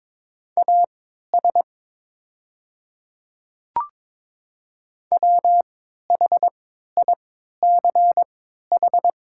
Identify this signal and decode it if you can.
Morse code, heard as a keyed tone, about 22 wpm, spaced as for 12 wpm, 700 Hz. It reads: AS WHICH
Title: 3x Character Spacing - Top 100 Words - 22wpm